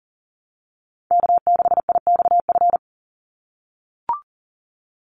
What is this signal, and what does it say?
Morse code, heard as a keyed tone, about 40 wpm, 700 Hz. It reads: K6IXF
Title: USA Call Signs 40wpm